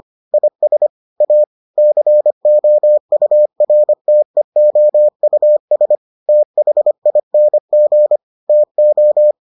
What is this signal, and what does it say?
Morse code, heard as a keyed tone, about 25 wpm, 600 Hz. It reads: IS A COURTEOUS THING TO